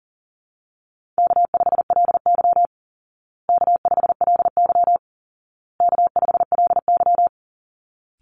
Morse code, heard as a keyed tone, about 40 wpm, 700 Hz. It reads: K5LY K5LY K5LY
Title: Easy Call Signs Repeated 3x - 40wpm